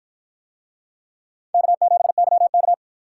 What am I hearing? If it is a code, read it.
Morse code, heard as a keyed tone, about 49 wpm, 700 Hz. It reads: K7YK